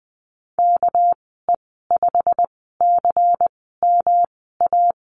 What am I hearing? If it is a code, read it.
Morse code, heard as a keyed tone, about 19 wpm, 700 Hz. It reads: K E 5 C M A
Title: 238 Lesson 38g - Letter q - Call Signs 20wpm